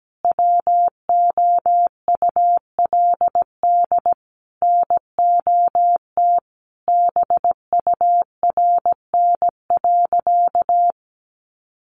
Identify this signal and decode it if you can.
Morse code, heard as a keyed tone, about 17 wpm, 700 Hz. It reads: WOULD NOT BURN.
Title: Sentences from Top 1300 Words 17wpm